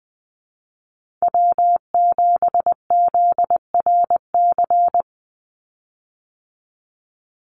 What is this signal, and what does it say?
Morse code, heard as a keyed tone, about 20 wpm, 700 Hz. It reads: W7ZRC
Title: Call Signs Repeated 3x - 20wpm